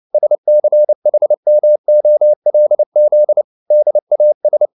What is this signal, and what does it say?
Morse code, heard as a keyed tone, about 29 wpm, 600 Hz. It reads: SCHMOLZ DAS